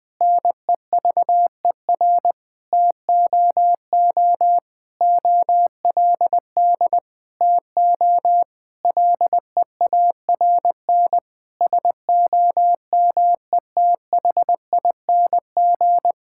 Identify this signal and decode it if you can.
Morse code, heard as a keyed tone, about 20 wpm, 700 Hz. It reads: NEVER TOO OLD TO LEARN SOMETHING
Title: Difficult Sentences with Hints - 20wpm